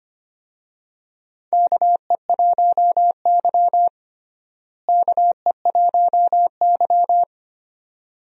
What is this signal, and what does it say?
Morse code, heard as a keyed tone, about 25 wpm, 700 Hz. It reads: KE1Y KE1Y
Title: USA Call Signs - 4 Characters - Repeated 2x - 25wpm